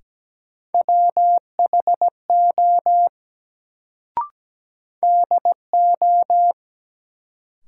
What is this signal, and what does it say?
Morse code, heard as a keyed tone, about 17 wpm, 700 Hz. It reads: WHO DO
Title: Sentences from Top 100 Words - New Word Drill - 17wpm